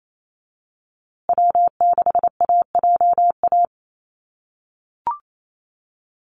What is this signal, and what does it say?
Morse code, heard as a keyed tone, about 28 wpm, 700 Hz. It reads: W6AJA
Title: USA Call Signs Repeated 2x 28wpm